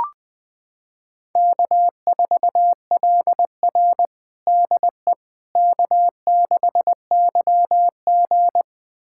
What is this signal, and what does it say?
Morse code, heard as a keyed tone, about 20 wpm, 700 Hz. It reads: K4LR DE K6YG
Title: USA Callsign DE Callsign - 4 Characters - 20wpm